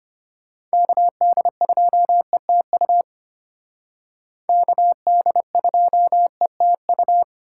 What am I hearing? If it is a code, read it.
Morse code, heard as a keyed tone, about 32 wpm, 700 Hz. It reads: KD2ETU KD2ETU